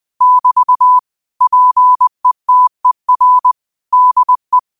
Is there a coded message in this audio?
Yes: Morse code from 0.2 to 4.6 s, heard as a keyed tone, about 20 wpm, 1000 Hz. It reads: <BT> PETER DE